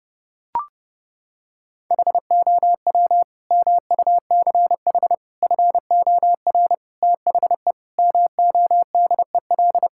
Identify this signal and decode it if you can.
Morse code, heard as a keyed tone, about 30 wpm, 700 Hz. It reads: HOW MUCH FOR THE MODEL
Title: Sentences from Top 1000 Words - Review Part 5 - 30wpm